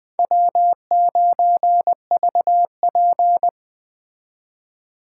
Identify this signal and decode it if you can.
Morse code, heard as a keyed tone, about 20 wpm, 700 Hz. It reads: W9VP
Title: Speed Racing - USA 1x2 Call Signs 17wpm